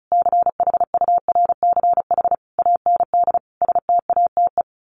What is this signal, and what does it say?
Morse code, heard as a keyed tone, about 34 wpm, 700 Hz. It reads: CHURCH AND STATE
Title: Sets of 5 Words 35wpm